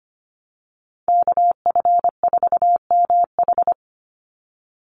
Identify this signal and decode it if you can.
Morse code, heard as a keyed tone, about 25 wpm, 700 Hz. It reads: KF4MH
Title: NA SOTA Chaser Callsigns 25wpm